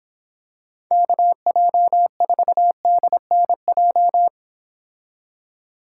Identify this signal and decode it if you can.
Morse code, heard as a keyed tone, about 26 wpm, 700 Hz. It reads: KJ4DNJ